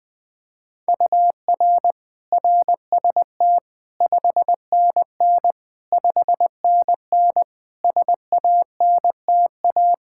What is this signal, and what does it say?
Morse code, heard as a keyed tone, about 21 wpm, 700 Hz. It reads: UR RST 5NN 5NN SANTA
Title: Practice QSO - Exchange 1 - Beginning - 20wpm